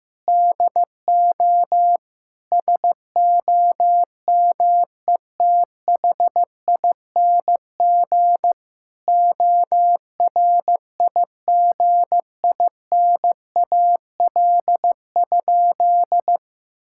Morse code, heard as a keyed tone, about 16 wpm, 700 Hz. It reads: DO SOMETHING ORIGINAL?
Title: Sentences from Top 1000 Words - Review Part 6 - 15wpm